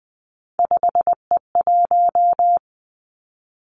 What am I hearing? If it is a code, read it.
Morse code, heard as a keyed tone, about 20 wpm, 700 Hz. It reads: 5E1
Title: Three Challenging Letter-Number Combinations 20wpm